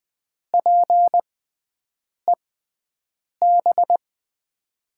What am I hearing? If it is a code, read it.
Morse code, heard as a keyed tone, about 20 wpm, spaced as for 7 wpm, 700 Hz. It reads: PEB
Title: USA Call Signs - 5 Characters - 6x Character Spacing - 20wpm